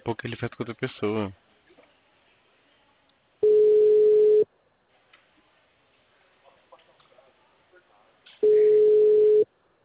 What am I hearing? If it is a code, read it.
Ringback tone (Europe, 425 Hz), with speech.